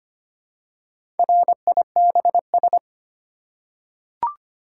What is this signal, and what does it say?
Morse code, heard as a keyed tone, about 25 wpm, 700 Hz. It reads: RIBS
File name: practice / 25wpm / Easy Words with Hints - 25wpm